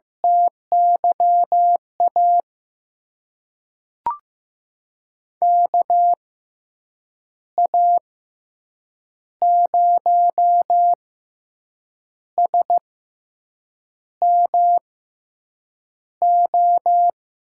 Morse code, heard as a keyed tone, about 15 wpm, spaced as for 6 wpm, 700 Hz. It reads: TYA KA0SMO
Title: USA Call Signs - 6 Characters - 6x Character Spacing - 15wpm